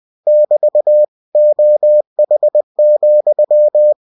Morse code, heard as a keyed tone, about 20 wpm, 600 Hz. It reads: <BT> OH,
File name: cw-books / winnie-the-pooh / winnie-20wpm-page-0016